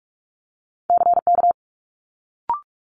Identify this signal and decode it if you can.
Morse code, heard as a keyed tone, about 46 wpm, 700 Hz. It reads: CK
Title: Multi-letter Phonemes Quick Fire 45wpm